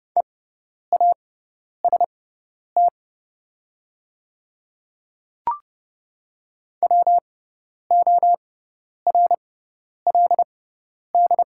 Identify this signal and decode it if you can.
Morse code, heard as a keyed tone, about 30 wpm, spaced as for 10 wpm, 690 Hz. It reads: EAST WORLD
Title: 6x Character Spacing - Top 500 Words - 30wpm